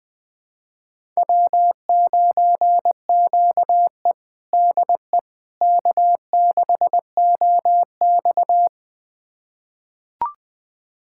Morse code, heard as a keyed tone, about 20 wpm, 700 Hz. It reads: W9QE DE K6OX
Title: USA Callsign DE Callsign - 4 Characters - 20wpm